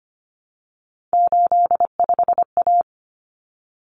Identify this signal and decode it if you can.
Morse code, heard as a keyed tone, about 25 wpm, 700 Hz. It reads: 85A
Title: Common Cut Numbers - 3-Digit 25wpm